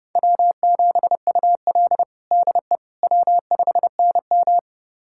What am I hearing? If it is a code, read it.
Morse code, heard as a keyed tone, about 29 wpm, 700 Hz. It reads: W7UL DE W5NM
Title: USA Callsign DE Callsign - 4 Characters - 30wpm